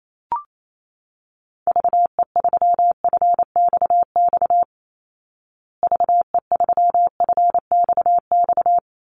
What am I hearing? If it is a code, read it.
Morse code, heard as a keyed tone, about 28 wpm, 700 Hz. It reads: VE3FXX VE3FXX
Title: Call Signs Repeated 2x - 28wpm